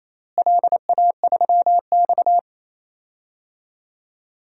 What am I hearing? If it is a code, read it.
Morse code, heard as a keyed tone, about 28 wpm, 700 Hz. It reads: LA3X
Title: DXCC-100 Call Signs - 4 Characters - Repeated 2x - 28wpm